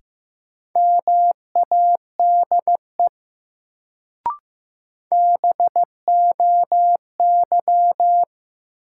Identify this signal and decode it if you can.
Morse code, heard as a keyed tone, about 16 wpm, 700 Hz. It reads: MADE BOY
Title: Sentences from Top 200 Words - New Word Drill - 15wpm